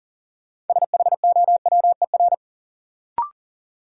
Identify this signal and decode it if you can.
Morse code, heard as a keyed tone, about 40 wpm, 700 Hz. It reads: SHOWER
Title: Sentences from Top 1600 Words - New Word Drill Alt - 40wpm